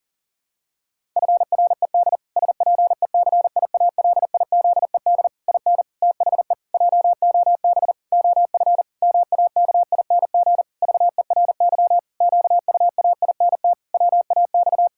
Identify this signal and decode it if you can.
Morse code, heard as a keyed tone, about 40 wpm, 700 Hz. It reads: FRED SPECIALIZED IN THE JOB OF MAKING VERY QUAINT WAX
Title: Pangram Challenge v2 - 40wpm